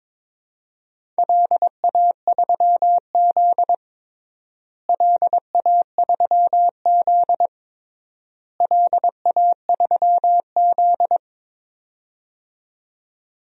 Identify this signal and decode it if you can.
Morse code, heard as a keyed tone, about 22 wpm, 700 Hz. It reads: LA3Z LA3Z LA3Z
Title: DXCC-100 Call Signs - 4 Characters - Repeated 3x - 22wpm